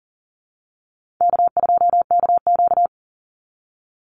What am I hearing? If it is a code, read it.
Morse code, heard as a keyed tone, about 40 wpm, 700 Hz. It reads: K2KQ
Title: Easy Call Signs - 40wpm